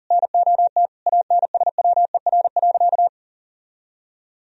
Morse code, heard as a keyed tone, about 40 wpm, 700 Hz. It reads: NOT ANSWER.
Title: Sentences from Top 1000 Words - Review Part 4 - 40wpm